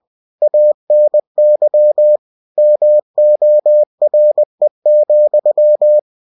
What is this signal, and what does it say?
Morse code, heard as a keyed tone, about 20 wpm, 600 Hz. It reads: ANY MORE,
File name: cw-books / winnie-the-pooh / winnie-20wpm-page-0062